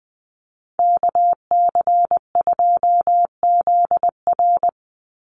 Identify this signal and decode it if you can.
Morse code, heard as a keyed tone, about 20 wpm, 700 Hz. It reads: KC2ZR
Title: USA Call Signs 20wpm